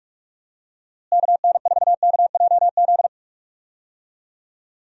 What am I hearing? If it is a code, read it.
Morse code, heard as a keyed tone, about 43 wpm, 700 Hz. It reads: KN4KJZ
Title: USA Call Signs - 6 Characters - Repeated 2x - 45wpm